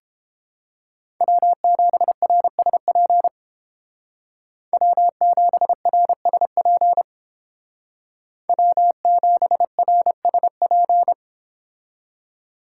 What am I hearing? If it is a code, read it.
Morse code, heard as a keyed tone, about 30 wpm, 700 Hz. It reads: W7RSP W7RSP W7RSP